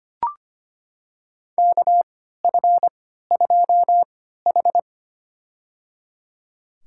Morse code, heard as a keyed tone, about 25 wpm, spaced as for 14 wpm, 700 Hz. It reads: KF2H
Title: USA Call Signs - 4 Characters - 3x Character Spacing - 25wpm